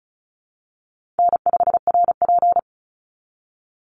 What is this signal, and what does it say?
Morse code, heard as a keyed tone, about 34 wpm, 700 Hz. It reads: N5RP
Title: Easy Call Signs Repeated 3x - 35wpm